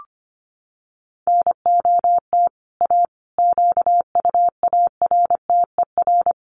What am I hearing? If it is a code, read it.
Morse code, heard as a keyed tone, about 25 wpm, 700 Hz. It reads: NOT A QUARTER
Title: Sentences from Top 1300 Words 25wpm